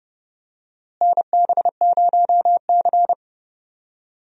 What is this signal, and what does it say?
Morse code, heard as a keyed tone, about 29 wpm, 700 Hz. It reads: NB0C
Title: USA Call Signs - 4 Characters - Repeated 3x - 30wpm